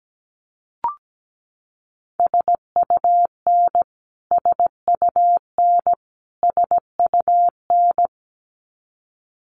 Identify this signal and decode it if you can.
Morse code, heard as a keyed tone, about 17 wpm, 700 Hz. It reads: SUN SUN SUN
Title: Top 300 Words Repeated 17wpm